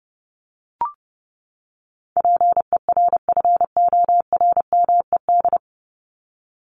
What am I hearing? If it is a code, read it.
Morse code, heard as a keyed tone, about 30 wpm, 700 Hz. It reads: PERFORMED
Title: ED Suffix - Top 100 Words with Base Words 30wpm